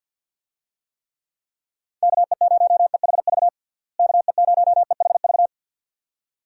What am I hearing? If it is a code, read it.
Morse code, heard as a keyed tone, about 52 wpm, 700 Hz. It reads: KE0EHV KE0EHV
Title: USA Call Signs - 6 Characters - Repeated 2x - 50wpm